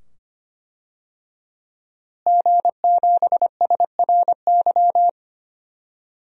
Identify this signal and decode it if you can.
Morse code, heard as a keyed tone, about 25 wpm, 700 Hz. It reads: G7SRY
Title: DXCC-100 Call Signs - 5 Characters - Repeated 2x - 25wpm